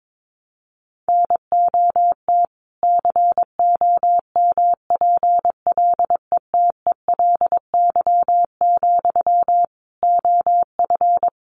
Morse code, heard as a keyed tone, about 22 wpm, 700 Hz. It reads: NOT COMPLETELY, OF
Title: Sentences from Top 1100 Words 22wpm